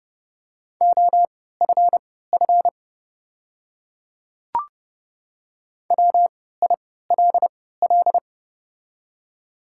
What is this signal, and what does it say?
Morse code, heard as a keyed tone, about 31 wpm, spaced as for 17 wpm, 700 Hz. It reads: OFF WILL
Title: 3x Character Spacing - Top 300 Words - 30wpm